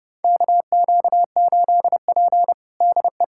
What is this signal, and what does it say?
Morse code, heard as a keyed tone, about 31 wpm, 700 Hz. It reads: KQ8P DE
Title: USA Callsign DE Callsign 30wpm